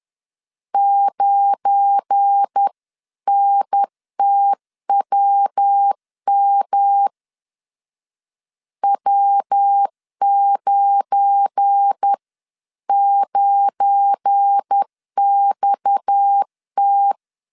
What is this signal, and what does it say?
Morse code, heard as a keyed tone, about 10 wpm, 800 Hz. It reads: 9NTWM W99XT
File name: LEZIONI / 7 GF9Z / 7 GF9 10wpm 2747